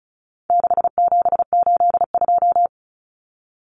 Morse code, heard as a keyed tone, about 34 wpm, 700 Hz. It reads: 6782
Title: Common Cut Numbers - 4-Digit 35wpm